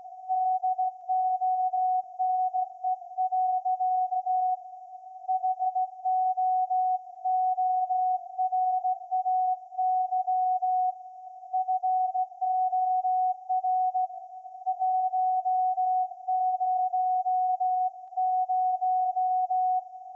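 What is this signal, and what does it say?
Morse code, heard as a keyed tone, about 16 wpm, 740 Hz. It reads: DONE. HOORAY FOR 100
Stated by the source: Morse code from SAQ, received at a station in Putignano, Apulia, Italy, on VLF.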